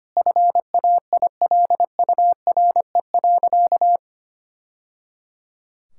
Morse code, heard as a keyed tone, about 25 wpm, 700 Hz. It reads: FAILURE.
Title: Sentences from Top 1000 Words - Review Part 6 - 25wpm